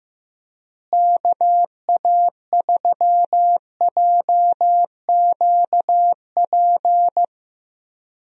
Morse code, heard as a keyed tone, about 15 wpm, 700 Hz. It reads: KA3JQP
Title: USA Call Signs - 6 Characters - Repeated 3x - 15wpm